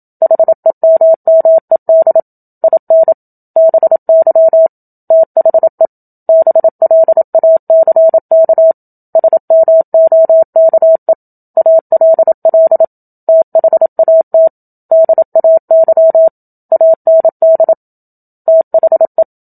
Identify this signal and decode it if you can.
Morse code, heard as a keyed tone, about 27 wpm, 650 Hz. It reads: HEMMED IN BY THE BLACK SMOKE ALL THAT DAY AND THE
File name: Book Two / 01-Chapter 1 - 27 WPM